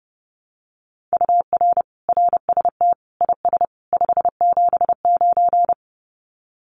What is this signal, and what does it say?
Morse code, heard as a keyed tone, about 28 wpm, 700 Hz. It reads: UR RST IS 579
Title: Common QSO Phrases 30wpm